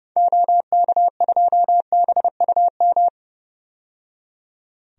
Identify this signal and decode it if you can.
Morse code, heard as a keyed tone, about 32 wpm, 700 Hz. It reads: OK2BUM
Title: DXCC-100 Call Signs - 6 Characters - 30wpm